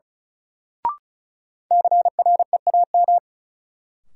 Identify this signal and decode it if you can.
Morse code, heard as a keyed tone, about 36 wpm, 700 Hz. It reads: CREAM
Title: Sentences from Top 1500 Words - New Word Drill Alt - 35wpm